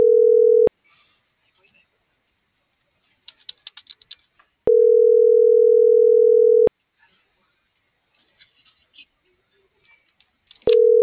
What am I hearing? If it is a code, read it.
Ringback tone (North America), 440 + 480 Hz.